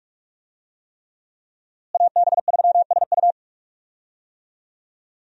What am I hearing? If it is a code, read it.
Morse code, heard as a keyed tone, about 41 wpm, 700 Hz. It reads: AB3SU
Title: USA Call Signs - 5 Characters - Repeated 3x - 45wpm